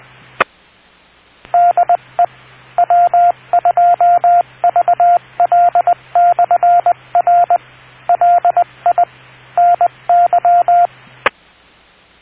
Morse code, heard as a keyed tone, about 20 wpm, 700 Hz, with radio noise. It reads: DE W2VL/R LI NY